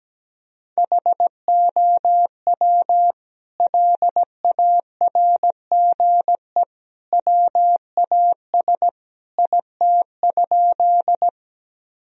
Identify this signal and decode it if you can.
Morse code, heard as a keyed tone, about 17 wpm, 700 Hz. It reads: HOW LARGE WAS IT?
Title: Sentences from Top 1000 Words - Review Part 1 - 17wpm